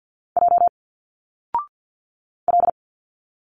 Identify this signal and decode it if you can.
Morse code, heard as a keyed tone, about 50 wpm, 700 Hz. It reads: J L
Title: Single Letters 50wpm